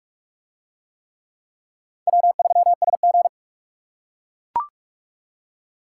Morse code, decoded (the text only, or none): W3SG